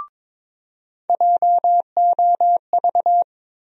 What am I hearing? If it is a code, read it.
Morse code, heard as a keyed tone, about 22 wpm, 700 Hz. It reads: JOV
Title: USA Call Sign Postfixes - 3 Letters 22wpm